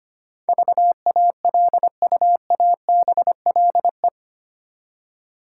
Morse code, heard as a keyed tone, about 25 wpm, 700 Hz. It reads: VALUABLE
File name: 8 Letter Words - First 3 Letters Given - 25wpm